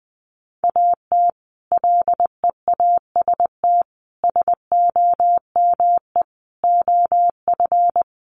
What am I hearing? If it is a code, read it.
Morse code, heard as a keyed tone, about 20 wpm, 700 Hz. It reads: AT LEAST SOME OF
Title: Sets of 5 Words Encore 20wpm